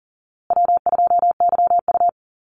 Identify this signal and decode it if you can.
Morse code, heard as a keyed tone, about 40 wpm, 700 Hz. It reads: W2YU
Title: USA Call Signs Repeated 3x 40wpm